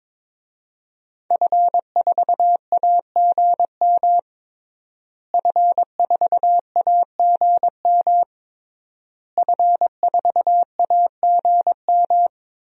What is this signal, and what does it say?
Morse code, heard as a keyed tone, about 22 wpm, 700 Hz. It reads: F4AGM F4AGM F4AGM
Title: DXCC-100 Call Signs - 5 Characters - Repeated 3x - 22wpm